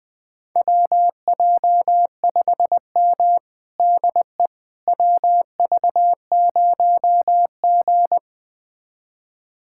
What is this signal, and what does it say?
Morse code, heard as a keyed tone, about 20 wpm, 700 Hz. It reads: WJ5M DE WV0G
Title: USA Callsign DE Callsign - 4 Characters - 20wpm